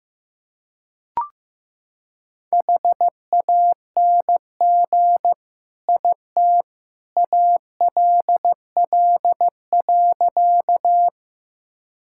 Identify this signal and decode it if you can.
Morse code, heard as a keyed tone, about 15 wpm, 700 Hz. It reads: HANG IT ALL.
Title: Sentences from Top 1600 Words 15wpm